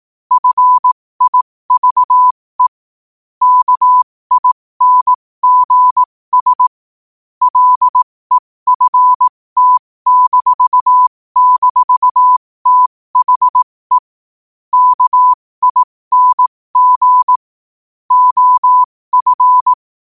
Morse code, heard as a keyed tone, about 18 wpm, spaced as for 15 wpm, 1000 Hz. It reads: FIVE KINGS LEFT--THE KING OF
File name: qotdfiles / 15wpm